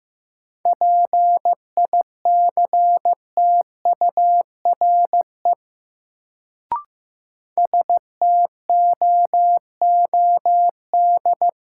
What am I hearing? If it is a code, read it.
Morse code, heard as a keyed tone, about 15 wpm, 700 Hz. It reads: PICTURE STOOD
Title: Sentences from Top 500 Words - New Word Drill - 15wpm